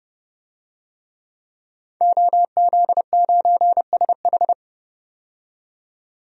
Morse code, heard as a keyed tone, about 30 wpm, 700 Hz. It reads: OZ9SH
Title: DXCC-100 Call Signs - 5 Characters - 30wpm